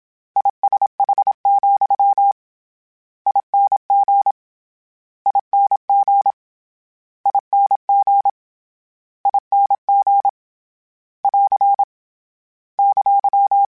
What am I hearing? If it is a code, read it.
Morse code, heard as a keyed tone, about 27 wpm, 800 Hz. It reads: ISH, ING ING ING ING <AR> !
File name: General / General-mp3s / General @25